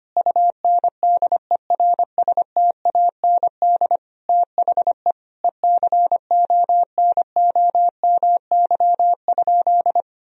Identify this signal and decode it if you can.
Morse code, heard as a keyed tone, about 25 wpm, 700 Hz. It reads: UNDERSTAND THE ECONOMY?
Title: Sentences from Top 1000 Words - Review Part 3 - 25wpm